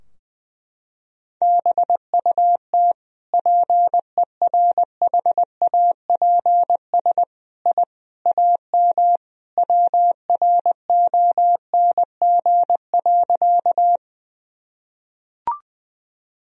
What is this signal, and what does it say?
Morse code, heard as a keyed tone, about 20 wpm, 700 Hz. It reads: BUT PERHAPS I AM WRONG.
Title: Sentences from Top 1000 Words - Review Part 2 - 20wpm